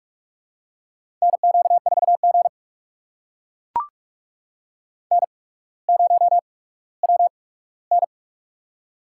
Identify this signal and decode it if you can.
Morse code, heard as a keyed tone, about 44 wpm, spaced as for 19 wpm, 700 Hz. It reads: NQ4G N0WN
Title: USA Call Signs - 4 Characters - 8x Character Spacing - 45wpm